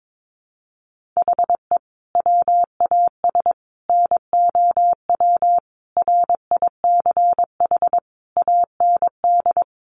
Morse code, heard as a keyed tone, about 22 wpm, 700 Hz. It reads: HE WAS NOW RICH AND
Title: Sentences from Top 1500 Words 22wpm